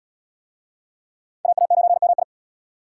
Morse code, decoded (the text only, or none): AA9MI